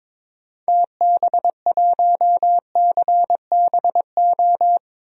Morse code, heard as a keyed tone, about 22 wpm, 700 Hz. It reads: TB1CBO
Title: DXCC-100 Call Signs - 6 Characters - Repeated 3x - 22wpm